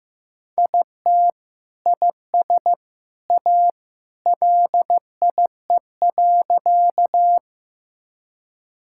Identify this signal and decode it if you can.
Morse code, heard as a keyed tone, about 15 wpm, 700 Hz. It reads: IT IS A LIE.
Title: Sentences from Top 1600 Words 15wpm